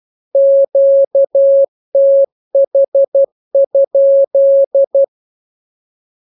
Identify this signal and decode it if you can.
Morse code, heard as a keyed tone, about 12 wpm, 550 Hz. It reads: QTH?
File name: jokes / joke 15 12 0000